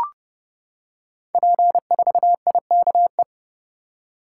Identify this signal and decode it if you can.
Morse code, heard as a keyed tone, about 30 wpm, 700 Hz. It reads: P4IKE
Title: Code Groups - 5 Letter:Numbers - 30wpm